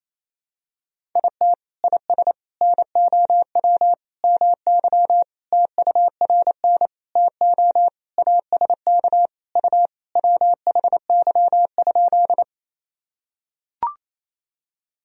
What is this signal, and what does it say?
Morse code, heard as a keyed tone, about 28 wpm, 700 Hz. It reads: IT IS NOW MY TURN TO ASK U WHY?
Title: Sentences from Top 200 Words and Abbreviations 28wpm